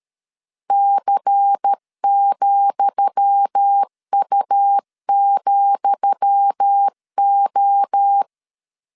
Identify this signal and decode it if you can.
Morse code, heard as a keyed tone, about 13 wpm, 800 Hz. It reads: C,U,O
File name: LEZIONI / 10 SIGNS / 10D COMA 12wpm 2544